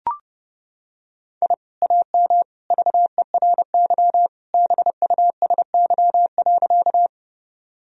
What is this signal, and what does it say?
Morse code, heard as a keyed tone, about 30 wpm, 700 Hz. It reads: I AM VERY BUSY.
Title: Sentences from Top 1000 Words - Review Part 3 - 30wpm